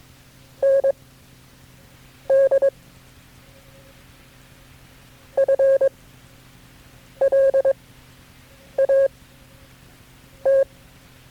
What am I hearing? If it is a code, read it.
Morse code, heard as a keyed tone, about 22 wpm, spaced as for 6 wpm, 540 Hz. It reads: ND FLAT
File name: FreeFiles / CW-practice / KB6MT-tape3-side1